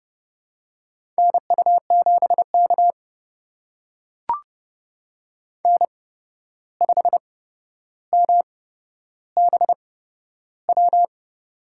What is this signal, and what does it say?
Morse code, heard as a keyed tone, about 30 wpm, spaced as for 12 wpm, 700 Hz. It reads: NU7K N5MBW
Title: USA Call Signs - 8x Character Spacing - 30wpm